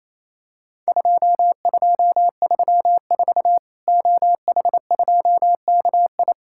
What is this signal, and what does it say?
Morse code, heard as a keyed tone, about 28 wpm, 700 Hz. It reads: 2234 OH2KI
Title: Worked All Europe DX Contest - QTC Exchange - 28wpm